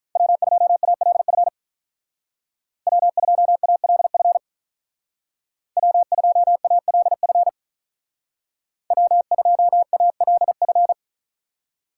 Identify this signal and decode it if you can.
Morse code, heard as a keyed tone, about 43 wpm, 700 Hz. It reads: W2ALF W2ALF W2ALF W2ALF
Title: Speed Racing - USA 1x3 Call Signs 35wpm